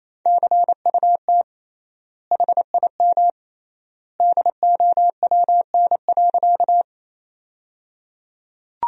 Morse code, heard as a keyed tone, about 28 wpm, 700 Hz. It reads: CUT HIM DOWN.
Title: 3x Word-Spacing - Sentences from Top 300 Words - 28wpm